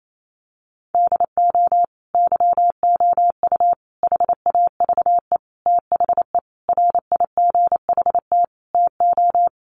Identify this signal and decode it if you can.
Morse code, heard as a keyed tone, about 28 wpm, 700 Hz. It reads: DO YOU HAVE THE RIGHT TO
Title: Sentences from Top 1400 Words 28wpm